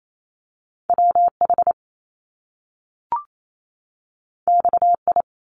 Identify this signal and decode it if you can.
Morse code, heard as a keyed tone, about 28 wpm, 700 Hz. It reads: WH XI
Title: Top Two Letter Combinations in Words 28wpm